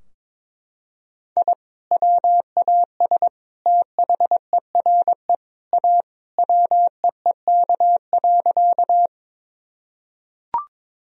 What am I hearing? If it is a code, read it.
Morse code, heard as a keyed tone, about 22 wpm, 700 Hz. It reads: I WAS THERE A WEEK.
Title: Sentences from Top 1000 Words - Review Part 2 - 22wpm